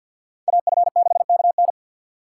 Morse code, heard as a keyed tone, about 52 wpm, 700 Hz. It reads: AF6KN
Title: USA Call Signs - 5 Characters - 2x Character Spacing - 50wpm